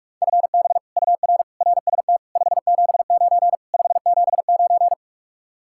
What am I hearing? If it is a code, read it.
Morse code, heard as a keyed tone, about 49 wpm, 700 Hz. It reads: FB UR RST 579 579